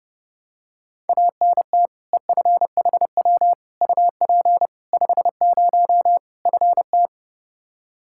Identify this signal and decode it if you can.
Morse code, heard as a keyed tone, about 32 wpm, 700 Hz. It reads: ANT EFHW UP 50 FT
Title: Practice QSO - Exchange 2 - Info - 30wpm